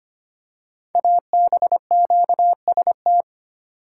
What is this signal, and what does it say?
Morse code, heard as a keyed tone, about 25 wpm, 700 Hz. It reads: ABQST
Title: Code Groups - 5 Letters - 25wpm